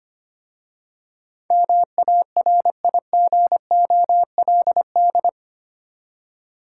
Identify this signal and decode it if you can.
Morse code, heard as a keyed tone, about 25 wpm, 700 Hz. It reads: MARIGOLD
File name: practice / 25wpm / Difficult Words with Hints - 25wpm